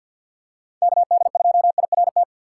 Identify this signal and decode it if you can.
Morse code, heard as a keyed tone, about 52 wpm, 700 Hz. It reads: KD2IRT
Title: USA Call Signs - 2x Character Spacing - 50wpm